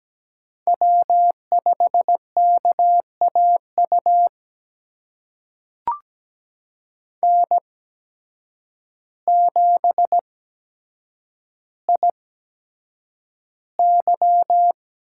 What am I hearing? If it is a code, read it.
Morse code, heard as a keyed tone, about 17 wpm, spaced as for 8 wpm, 700 Hz. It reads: W5KAU N7IY
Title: USA Call Signs - 8x Character Spacing - 17wpm